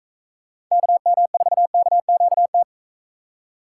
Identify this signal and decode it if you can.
Morse code, heard as a keyed tone, about 38 wpm, 700 Hz. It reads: KM4KQT